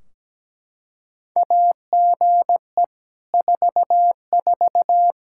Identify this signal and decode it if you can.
Morse code, heard as a keyed tone, about 17 wpm, 700 Hz. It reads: AGE 44
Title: Practice QSO - Exchange 3 - More Info - 17wpm